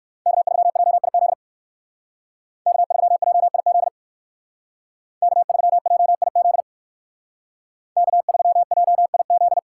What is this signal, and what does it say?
Morse code, heard as a keyed tone, about 54 wpm, 700 Hz. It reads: K3JIZ K3JIZ K3JIZ K3JIZ
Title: Speed Racing - USA Call Signs 45wpm